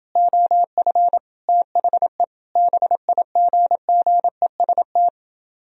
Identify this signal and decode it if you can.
Morse code, heard as a keyed tone, about 31 wpm, 700 Hz. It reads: OF THE BIGGEST